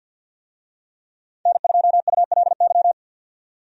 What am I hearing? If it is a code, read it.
Morse code, heard as a keyed tone, about 52 wpm, 700 Hz. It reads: N2ULY
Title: USA Call Signs - 5 Characters - 50wpm